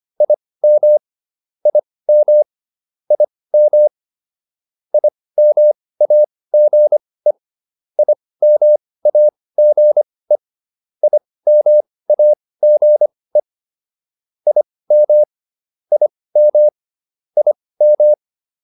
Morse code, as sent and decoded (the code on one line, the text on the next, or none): .. -- / .. -- / .. -- / .. -- .- --. / .. -- .- --. / .. -- .- --. / .. -- / .. -- / .. --
IM IM IM IMAGE IMAGE IMAGE IM IM IM